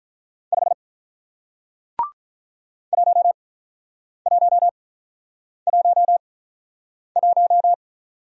Morse code, heard as a keyed tone, about 47 wpm, spaced as for 9 wpm, 700 Hz. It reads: 5 1111